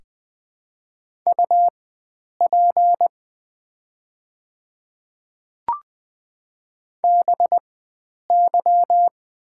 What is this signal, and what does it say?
Morse code, heard as a keyed tone, about 20 wpm, spaced as for 9 wpm, 690 Hz. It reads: UP BY
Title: 4x Character Spacing - Top 100 Words - 20wpm